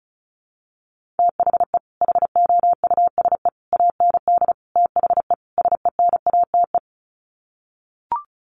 Morse code, heard as a keyed tone, about 34 wpm, 700 Hz. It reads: THE HOUSE AND THE SENATE
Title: Sets of 5 Words Encore2 35wpm